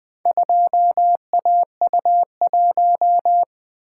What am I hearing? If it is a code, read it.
Morse code, heard as a keyed tone, about 20 wpm, 700 Hz. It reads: 2AU1
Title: Four Letter-Number Combinations - 20wpm